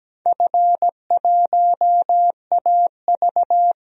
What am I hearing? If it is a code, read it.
Morse code, heard as a keyed tone, about 17 wpm, 700 Hz. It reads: F1AV
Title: DXCC-100 Call Signs - 4 Characters - Repeated 3x - 17wpm